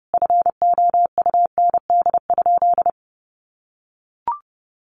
Morse code, heard as a keyed tone, about 30 wpm, 700 Hz. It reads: FOUND?
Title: Sentences from Top 300 Words 30wpm